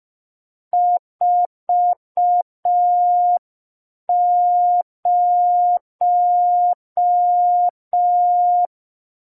Morse code, heard as a keyed tone, about 5 wpm, 710 Hz. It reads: 40
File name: g3zpd / Morse code P2